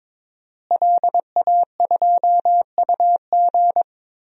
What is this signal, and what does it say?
Morse code, heard as a keyed tone, about 21 wpm, 700 Hz. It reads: LA2UG
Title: DXCC-100 Call Signs - 5 Characters - Repeated 2x - 22wpm